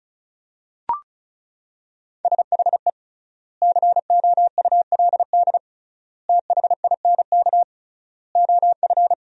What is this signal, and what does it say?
Morse code, heard as a keyed tone, about 36 wpm, 700 Hz. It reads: SHE COULD THINK OF
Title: 3x Word-Spacing - Sentences from Top 500 Words - 35wpm